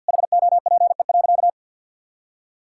Morse code, heard as a keyed tone, about 50 wpm, 700 Hz. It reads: HOPE.
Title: Sentences from Top 1000 Words - Review Part 6 - 50wpm